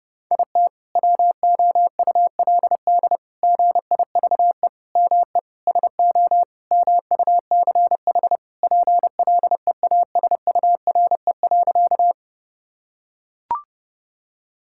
Morse code, heard as a keyed tone, about 30 wpm, 700 Hz. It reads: IT WOULD GIVE ME SO MUCH PLEASURE.
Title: Sentences from Top 1600 Words 30wpm